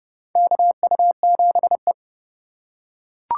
Morse code, heard as a keyed tone, about 31 wpm, 700 Hz. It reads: KU7E
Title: USA Call Signs - 4x Character Spacing - 30wpm